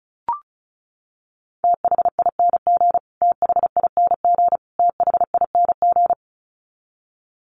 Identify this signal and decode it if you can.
Morse code, heard as a keyed tone, about 34 wpm, 700 Hz. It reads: THING THING THING
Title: Top 300 Words Repeated 35wpm